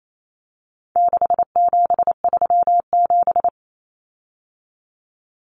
Morse code, heard as a keyed tone, about 28 wpm, 700 Hz. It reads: 6737
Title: Common Cut Numbers - 4-Digit 28wpm